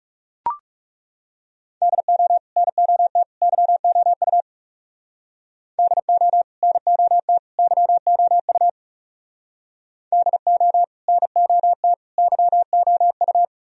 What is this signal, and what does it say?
Morse code, heard as a keyed tone, about 36 wpm, 700 Hz. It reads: DO NOT YOU DO NOT YOU DO NOT YOU
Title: Speed-Racing - Sets of 3 Words 30wpm